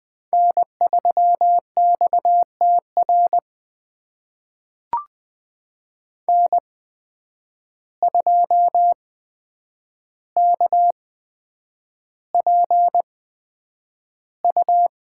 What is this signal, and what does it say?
Morse code, heard as a keyed tone, about 20 wpm, spaced as for 9 wpm, 700 Hz. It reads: N3XTR N2KPU
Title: USA Call Signs - 5 Characters - 8x Character Spacing - 20wpm